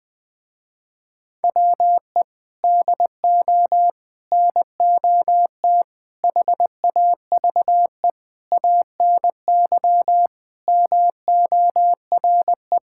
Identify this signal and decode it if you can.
Morse code, heard as a keyed tone, about 21 wpm, 700 Hz. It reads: WE DO NOT HAVE ANY MORE
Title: Sentences with 5-Word Phrases - Spoken Beginning - 20wpm